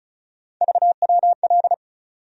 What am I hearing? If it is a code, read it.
Morse code, heard as a keyed tone, about 34 wpm, 700 Hz. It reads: VWL
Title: USA Call Sign Postfixes - 3 Letters 35wpm